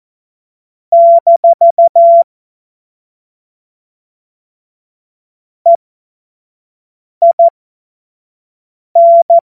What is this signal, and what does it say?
Morse code, heard as a keyed tone, about 14 wpm, spaced as for 5 wpm, 680 Hz. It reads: - EIN
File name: Morsekurs / Witze / Witz 21. Dezember 2021, 25 F70